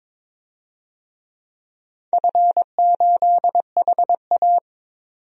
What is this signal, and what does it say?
Morse code, heard as a keyed tone, about 22 wpm, 700 Hz. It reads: F8HA
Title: DXCC-100 Call Signs - 4 Characters - 22wpm